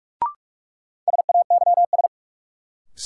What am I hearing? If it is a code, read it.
Morse code, heard as a keyed tone, about 46 wpm, 700 Hz. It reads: SAYS
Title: Sets of 2 Words - New Word Drill - 45wpm